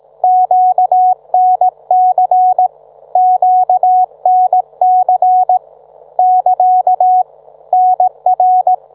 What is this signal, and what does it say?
Morse code, heard as a keyed tone, about 18 wpm, 740 Hz. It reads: QNC QNC <KA> NR